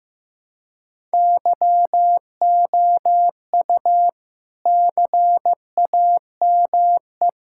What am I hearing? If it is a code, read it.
Morse code, heard as a keyed tone, about 15 wpm, 700 Hz. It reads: YOU CAME
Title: Sentences from Top 1000 Words - Review Part 6 - 15wpm